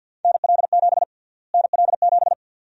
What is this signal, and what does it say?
Morse code, heard as a keyed tone, about 49 wpm, 700 Hz. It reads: NL7 NL7
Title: USA Call Sign Prefixes - Repeated 2x - 50wpm